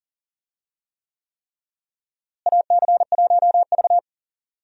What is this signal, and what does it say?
Morse code, heard as a keyed tone, about 40 wpm, 700 Hz. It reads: AC1V
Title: USA Call Signs - 4 Characters - Repeated 3x - 40wpm